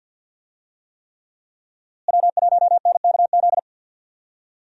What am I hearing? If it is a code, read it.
Morse code, heard as a keyed tone, about 48 wpm, 700 Hz. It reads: W1NKZ
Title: USA Call Signs - 5 Characters - Repeated 3x - 50wpm